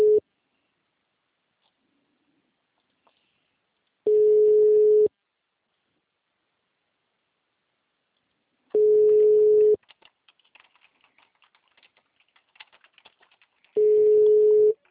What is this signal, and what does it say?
Ringback tone (Europe, 425 Hz).
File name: out- -2002- - - .3410